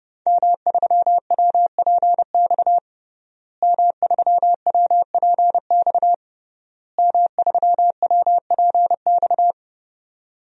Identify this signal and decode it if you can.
Morse code, heard as a keyed tone, about 30 wpm, 700 Hz. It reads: M3WPX M3WPX M3WPX
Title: DXCC-100 Call Signs - 5 Characters - Repeated 3x - 30wpm